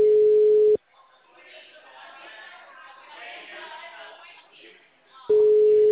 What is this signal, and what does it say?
Ringback tone (Europe, 425 Hz), with speech.